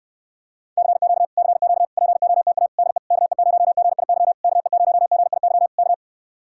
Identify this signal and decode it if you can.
Morse code, heard as a keyed tone, about 69 wpm, 700 Hz. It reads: CQ CQ POTA DE KE0GIY KE0GIY K